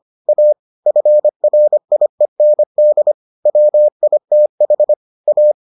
Morse code, heard as a keyed tone, about 25 wpm, 600 Hz. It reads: A FRIEND WITH A